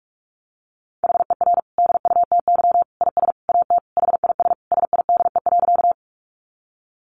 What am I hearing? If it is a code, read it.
Morse code, heard as a keyed tone, about 44 wpm, 700 Hz. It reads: HER DUTY IS AT HIS SIDE.
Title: Sentences from Top 1500 Words 45wpm